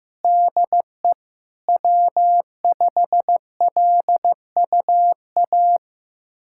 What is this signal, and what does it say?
Morse code, heard as a keyed tone, about 15 wpm, 700 Hz. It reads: DE W5LUA
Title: USA Callsign DE Callsign - 5 Characters - 15wpm